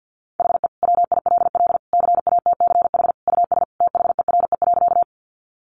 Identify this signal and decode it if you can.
Morse code, heard as a keyed tone, about 50 wpm, 700 Hz. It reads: HE WILL CATCH US THERE.
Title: Sentences from Top 1500 Words 50wpm